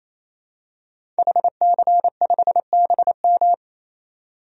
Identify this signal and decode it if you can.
Morse code, heard as a keyed tone, about 28 wpm, 700 Hz. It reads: HC5BM
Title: DXCC-100 Call Signs - 5 Characters - 28wpm